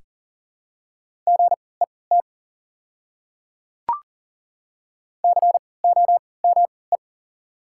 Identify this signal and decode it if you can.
Morse code, heard as a keyed tone, about 38 wpm, spaced as for 22 wpm, 700 Hz. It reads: GET COME
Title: 3x Character Spacing - Top 200 Words - 40wpm